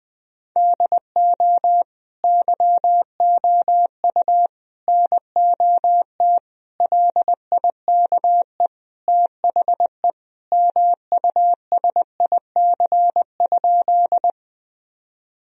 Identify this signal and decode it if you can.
Morse code, heard as a keyed tone, about 20 wpm, 700 Hz. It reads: DO YOU NOT LIKE THE MUSIC?
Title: Sentences from Top 1000 Words - Review Part 1 - 20wpm